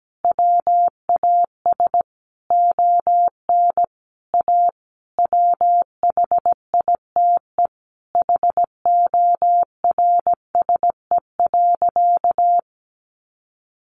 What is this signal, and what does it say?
Morse code, heard as a keyed tone, about 17 wpm, 700 Hz. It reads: WAS ON A WHITE HORSE.
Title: Sentences from Top 300 Words 17wpm